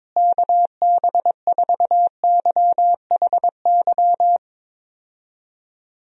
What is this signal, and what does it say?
Morse code, heard as a keyed tone, about 22 wpm, 700 Hz. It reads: KB4YHY
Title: USA Call Signs - 6 Characters - Repeated 3x - 22wpm